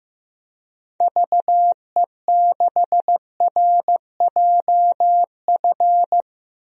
Morse code, heard as a keyed tone, about 15 wpm, 700 Hz. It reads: VE6RJF
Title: DXCC-100 Call Signs - 6 Characters - 15wpm